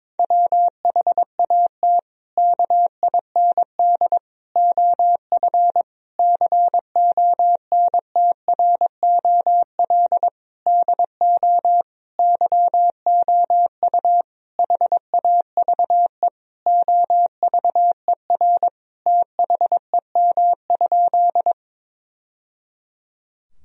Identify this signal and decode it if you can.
Morse code, heard as a keyed tone, about 22 wpm, 700 Hz. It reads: WHAT KIND OF CONTROL DO YOU HAVE OVER THEM?
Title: Sentences from Top 1000 Words - Review Part 3 - 22wpm